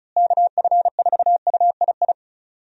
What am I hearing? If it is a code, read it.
Morse code, heard as a keyed tone, about 36 wpm, 700 Hz. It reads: KF4UII
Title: USA Call Signs - 6 Characters - Repeated 2x - 35wpm